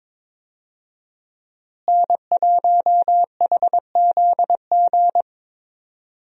Morse code, heard as a keyed tone, about 22 wpm, 700 Hz. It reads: N1HZG
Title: USA Call Signs - 5 Characters - Repeated 3x - 22wpm